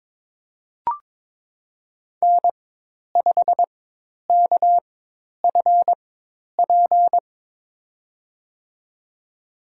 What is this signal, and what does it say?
Morse code, heard as a keyed tone, about 23 wpm, spaced as for 10 wpm, 700 Hz. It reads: N5KFP
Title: USA Call Signs - 5 Characters - 4x Character Spacing - 22wpm